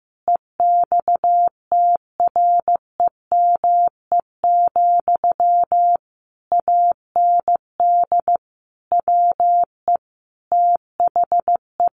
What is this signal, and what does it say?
Morse code, heard as a keyed tone, about 14 wpm, 700 Hz. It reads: EXTREME, AND WE THE
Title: Sentences from Top 1300 Words 15wpm